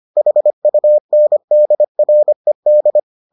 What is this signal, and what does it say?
Morse code, heard as a keyed tone, about 25 wpm, 600 Hz. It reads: HUNDRED